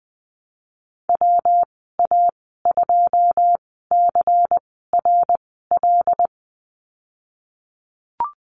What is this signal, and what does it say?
Morse code, heard as a keyed tone, about 20 wpm, spaced as for 15 wpm, 700 Hz. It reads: WA2CRL
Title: 161 Lesson 26g - Letter g - Call Signs 20wpm